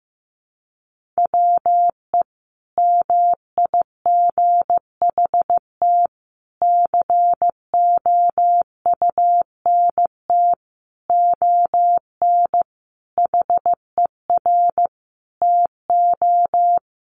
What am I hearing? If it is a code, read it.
Morse code, heard as a keyed tone, about 15 wpm, 700 Hz. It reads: WE MIGHT COUNT ON HER TO
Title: Sentences from Top 1500 Words 15wpm